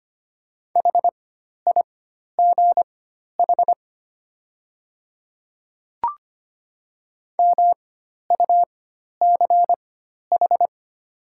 Morse code, heard as a keyed tone, about 25 wpm, spaced as for 12 wpm, 700 Hz. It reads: HIGH MUCH
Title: 4x Character Spacing - Top 300 Words - 25wpm